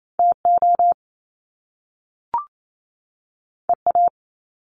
Morse code, heard as a keyed tone, about 28 wpm, 700 Hz. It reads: TO EA
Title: Top Two Letter Combinations in Words 28wpm